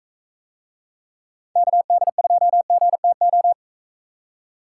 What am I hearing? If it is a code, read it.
Morse code, heard as a keyed tone, about 43 wpm, 700 Hz. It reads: KD2GTO